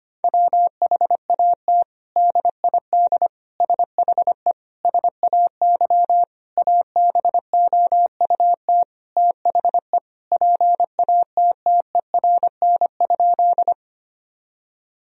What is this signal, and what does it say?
Morse code, heard as a keyed tone, about 25 wpm, 700 Hz. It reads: WHAT DID SHE SAY ABOUT THE PATTERN?
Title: Sentences from Top 1000 Words - Review Part 2 - 25wpm